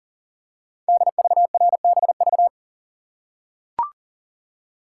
Morse code, heard as a keyed tone, about 39 wpm, 700 Hz. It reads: DVRBV
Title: Code Groups - 5 Letters - 40wpm